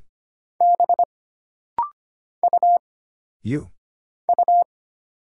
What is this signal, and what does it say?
Morse code, heard as a keyed tone, about 25 wpm, 700 Hz. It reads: B U U
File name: Single Character (Letter:Number:Punctuation:Prosign) - Rapid-Fire - 25wpm